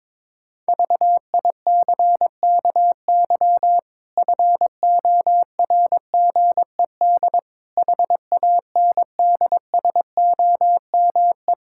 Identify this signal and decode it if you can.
Morse code, heard as a keyed tone, about 21 wpm, 700 Hz. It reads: VICKY FORGED HANDSOME
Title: Pangram Challenge v2 - 22wpm